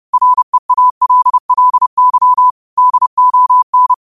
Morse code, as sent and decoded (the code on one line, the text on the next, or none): .-. .- .-.. .-.. -.-- / -.. --- -.
REALLY DON